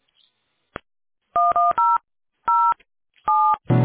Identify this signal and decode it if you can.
DTMF, heard as telephone touch tones, tones of about 280 ms. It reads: 1007